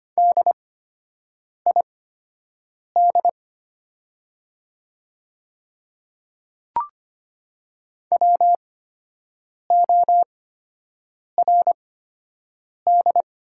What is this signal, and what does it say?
Morse code, heard as a keyed tone, about 25 wpm, spaced as for 7 wpm, 700 Hz. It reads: DID WORD
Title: 8x Character Spacing - Top 200 Words - 25wpm